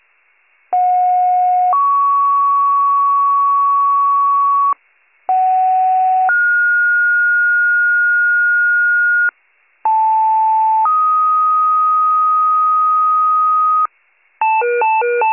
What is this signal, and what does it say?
Two-tone page, heard as a one-second tone, then a three-second tone, about 730 then 1120 Hz, about 750 then 1430 Hz, about 880 then 1250 Hz. It is heard from 0.7 to 13.9 s.